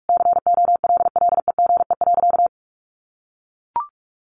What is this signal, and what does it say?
Morse code, heard as a keyed tone, about 44 wpm, 700 Hz. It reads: COLLEGE.
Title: Sentences from Top 1300 Words 45wpm